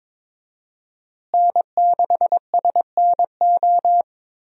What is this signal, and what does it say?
Morse code, heard as a keyed tone, about 22 wpm, 700 Hz. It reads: N6SNO
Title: USA Call Signs - 5 Characters - Repeated 2x - 22wpm